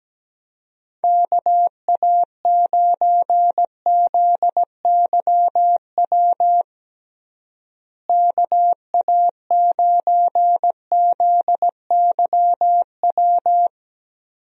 Morse code, heard as a keyed tone, about 17 wpm, 700 Hz. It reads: KA9ZYW KA9ZYW
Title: USA Call Signs - 6 Characters - Repeated 3x - 17wpm